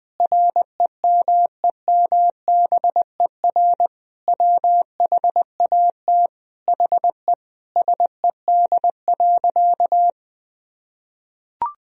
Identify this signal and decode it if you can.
Morse code, heard as a keyed tone, about 21 wpm, 700 Hz. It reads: REMEMBER WHAT HE SED.
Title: Sentences from Top 400 Words and Abbreviations 20wpm